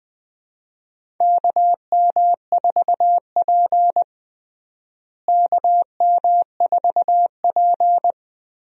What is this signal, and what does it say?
Morse code, heard as a keyed tone, about 21 wpm, 700 Hz. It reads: KM4P KM4P
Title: USA Call Signs - 4 Characters - Repeated 3x - 20wpm